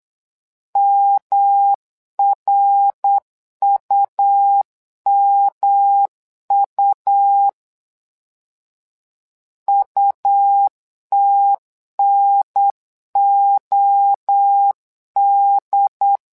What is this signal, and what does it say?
Morse code, heard as a keyed tone, about 8 wpm, 800 Hz. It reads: MRUMU UTNOD
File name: LEZIONI / 3 MWRO / 3 MWRO 8wpm 2606